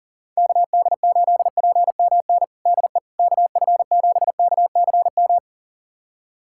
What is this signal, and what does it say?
Morse code, heard as a keyed tone, about 40 wpm, 700 Hz. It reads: KD8PMN DE KF7KCM
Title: USA Callsign DE Callsign - 6 Characters - 40wpm